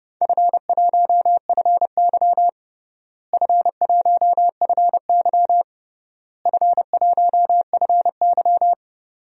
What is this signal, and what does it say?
Morse code, heard as a keyed tone, about 30 wpm, 700 Hz. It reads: F1FY F1FY F1FY
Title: DXCC-100 Call Signs - 4 Characters - Repeated 3x - 30wpm